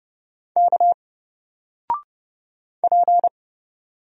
Morse code, heard as a keyed tone, about 31 wpm, 700 Hz. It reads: K P
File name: Single-Letter-Number-30wpm